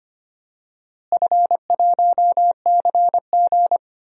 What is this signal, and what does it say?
Morse code, heard as a keyed tone, about 25 wpm, 700 Hz. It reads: F1CG
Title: DXCC-100 Call Signs - 4 Characters - Repeated 2x - 25wpm